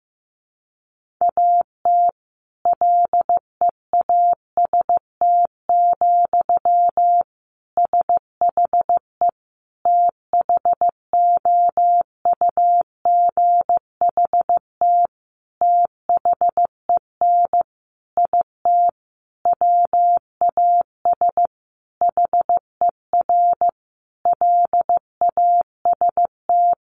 Morse code, heard as a keyed tone, about 15 wpm, 700 Hz. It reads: AT LEAST, SHE THOUGHT THEN IT WAS HER LAST
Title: Sentences from Top 1400 Words 15wpm